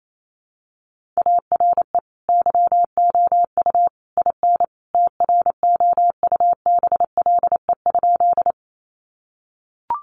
Morse code, heard as a keyed tone, about 28 wpm, 700 Hz. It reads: ARE YOU IN TROUBLE?
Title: Sentences from Top 1200 Words 28wpm